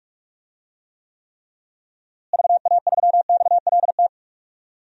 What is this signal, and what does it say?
Morse code, heard as a keyed tone, about 46 wpm, 700 Hz. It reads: VA3XLT
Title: DXCC-100 Call Signs - 6 Characters - Repeated 3x - 45wpm